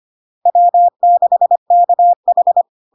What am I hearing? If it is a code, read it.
Morse code, heard as a keyed tone, about 25 wpm, 700 Hz. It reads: W6KH
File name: Log-in-CW-25wpm-700-Hz